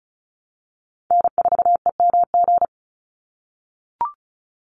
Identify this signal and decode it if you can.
Morse code, heard as a keyed tone, about 34 wpm, 700 Hz. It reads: N4EMG
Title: NA SOTA Chaser Callsigns 35wpm